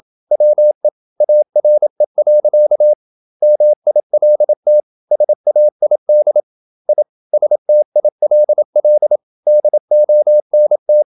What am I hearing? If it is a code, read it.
Morse code, heard as a keyed tone, about 27 wpm, 600 Hz. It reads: WE ARE. MILT SAID I STILL DONT
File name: Never 0027wpm